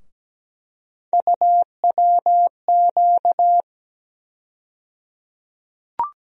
Morse code, heard as a keyed tone, about 17 wpm, 700 Hz. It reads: UWQ
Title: USA Call Sign Postfixes - 3 Letters 17wpm